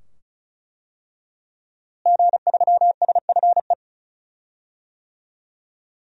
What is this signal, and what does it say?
Morse code, heard as a keyed tone, about 34 wpm, 700 Hz. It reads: G3SFE
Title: DXCC-100 Call Signs - 5 Characters - Repeated 2x - 35wpm